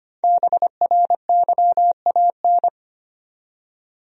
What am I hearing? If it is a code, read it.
Morse code, heard as a keyed tone, about 25 wpm, 700 Hz. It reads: BRYAN
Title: USA Common Names - 25wpm